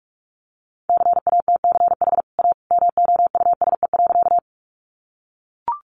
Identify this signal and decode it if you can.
Morse code, heard as a keyed tone, about 44 wpm, 700 Hz. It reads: CATCH A MOUSE.
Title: Sentences from Top 1500 Words 45wpm